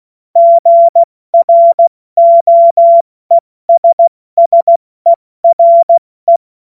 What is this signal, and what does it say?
Morse code, heard as a keyed tone, about 16 wpm, 680 Hz. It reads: GROESSERE
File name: Morsekurs / Witze / Witz 31. August 2021, 70 F80